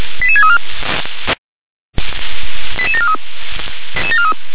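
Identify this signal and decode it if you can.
Five-tone selective call (ZVEI), heard as quick runs of five tones, tones of about 80 ms. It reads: 88634 88743
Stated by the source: ZVEI five-tone call received by SDR.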